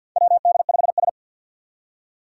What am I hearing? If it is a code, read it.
Morse code, heard as a keyed tone, about 47 wpm, 700 Hz. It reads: WD5S
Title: Four Challenging Letter-Number Combinations 50wpm